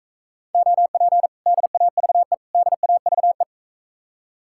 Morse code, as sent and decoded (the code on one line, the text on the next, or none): --- .--. / -.. .- ...- . / -.. .- ...- .
OP DAVE DAVE